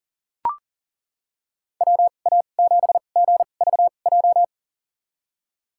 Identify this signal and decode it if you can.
Morse code, heard as a keyed tone, about 40 wpm, spaced as for 29 wpm, 700 Hz. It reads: WA7GVJ
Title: USA Call Signs - 2x Character Spacing - 40wpm